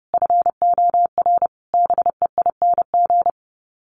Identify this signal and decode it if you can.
Morse code, heard as a keyed tone, about 30 wpm, 700 Hz. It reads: FOR BEING
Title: Sets of 5 Words Encore 30wpm